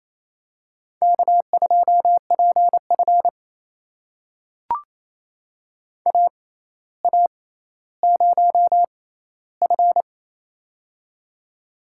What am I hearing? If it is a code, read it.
Morse code, heard as a keyed tone, about 28 wpm, spaced as for 14 wpm, 700 Hz. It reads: K2PF AA0F
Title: USA Call Signs - 4 Characters - 6x Character Spacing - 28wpm